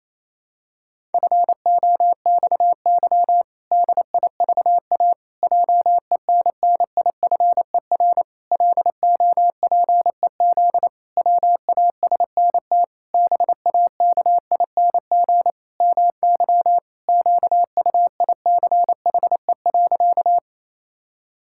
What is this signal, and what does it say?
Morse code, heard as a keyed tone, about 28 wpm, 700 Hz. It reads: FOXY DIVA JENNIFER LOPEZ WASNT BAKING MY QUICHE.
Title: Pangram Challenge v2 - 28wpm